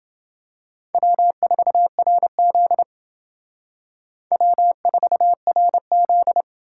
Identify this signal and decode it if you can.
Morse code, heard as a keyed tone, about 29 wpm, 700 Hz. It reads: W4RZ W4RZ